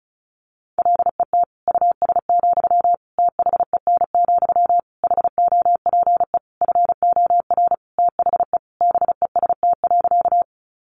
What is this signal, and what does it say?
Morse code, heard as a keyed tone, about 34 wpm, 700 Hz. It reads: LET US, THEN, HOPE FOR THE BEST.
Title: Sentences from Top 1000 Words 35wpm